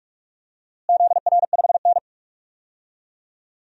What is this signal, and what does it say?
Morse code, heard as a keyed tone, about 42 wpm, 700 Hz. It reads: ZR5N
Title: DXCC-100 Call Signs - 4 Characters - Repeated 3x - 45wpm